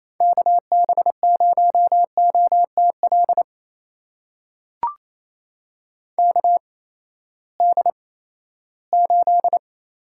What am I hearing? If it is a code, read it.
Morse code, heard as a keyed tone, about 28 wpm, spaced as for 16 wpm, 700 Hz. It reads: KB0OTL KD8
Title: USA Call Signs - 6 Characters - 8x Character Spacing - 28wpm